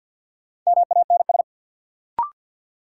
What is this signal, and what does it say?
Morse code, heard as a keyed tone, about 50 wpm, 700 Hz. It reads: MANS
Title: Sentences from Top 1000 Words - Review Part 6 - New Word Drill - 50wpm